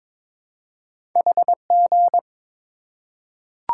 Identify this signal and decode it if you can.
Morse code, heard as a keyed tone, about 22 wpm, 690 Hz. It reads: HG
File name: practice / 22wpm / Two Letter-Number Combinations - 22wpm